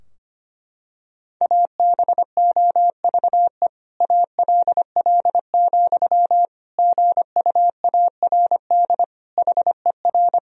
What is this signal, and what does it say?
Morse code, heard as a keyed tone, about 25 wpm, 700 Hz. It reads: ABOVE ALL, GUARD HER
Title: Sentences from Top 1000 Words - Review Part 4 - 25wpm